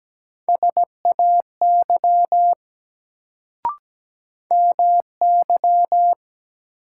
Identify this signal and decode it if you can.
Morse code, heard as a keyed tone, about 17 wpm, 700 Hz. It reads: SAY MY
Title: Sentences from Top 200 Words - New Word Drill - 17wpm